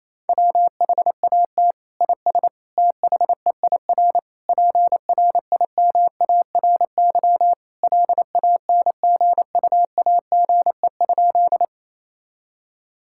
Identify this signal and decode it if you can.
Morse code, heard as a keyed tone, about 28 wpm, 700 Hz. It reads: WHAT IS THEIR PRIMARY LANGUAGE?
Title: Sentences from Top 1000 Words - Review Part 5 - 28wpm